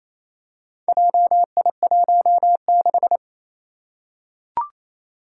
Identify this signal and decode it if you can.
Morse code, heard as a keyed tone, about 28 wpm, 700 Hz. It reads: JI16
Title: Four Challenging Letter-Number Combinations 28wpm